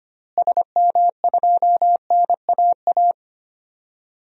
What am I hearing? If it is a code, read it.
Morse code, heard as a keyed tone, about 25 wpm, 700 Hz. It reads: SM2NAA
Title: DXCC-100 Call Signs - 6 Characters - Repeated 3x - 25wpm